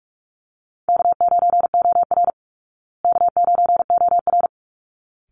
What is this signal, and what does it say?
Morse code, heard as a keyed tone, about 44 wpm, 700 Hz. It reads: K9OR K9OR
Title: Easy Call Signs Repeated 2x - 45wpm